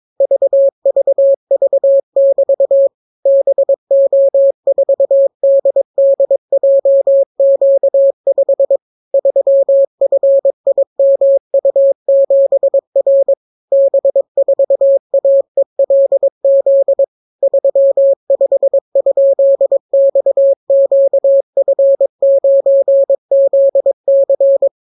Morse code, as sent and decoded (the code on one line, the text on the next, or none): ...- ...- ...- -...- / -... --- ....- -.. -.. .--- --.- ..... / ...-- ..-. .. -- ..- --... .-. / -... ....- .- . .-.. --.. / ...-- ..... ..--.. -..- --.- ..-. ----. --.. -.-.
VVV<BT> BO4DDJQ5 3FIMU7R B4AELZ 35?XQF9ZC